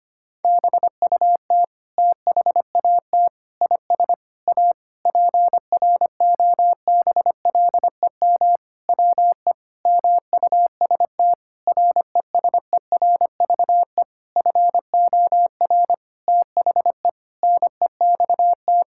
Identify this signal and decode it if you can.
Morse code, heard as a keyed tone, about 25 wpm, 700 Hz. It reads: BUT THAT IS A PROBLEM WE MUST RESERVE FOR THE NEXT
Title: Sentences from Top 1000 Words - Review Part 6 - 25wpm